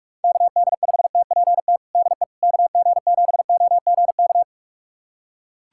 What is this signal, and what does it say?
Morse code, heard as a keyed tone, about 41 wpm, 700 Hz. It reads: KD5TPT DE KG7OGK
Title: USA Callsign DE Callsign 45wpm